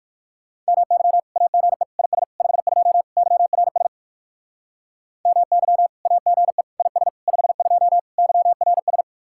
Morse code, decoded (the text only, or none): MY AGE IS 52 YRS MY AGE IS 52 YRS